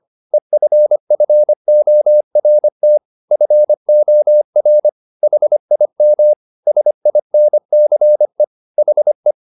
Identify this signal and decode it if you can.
Morse code, heard as a keyed tone, about 25 wpm, 600 Hz. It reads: EFFORT FOR HIM SINCE HE
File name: the old man and the sea part 1 25wpm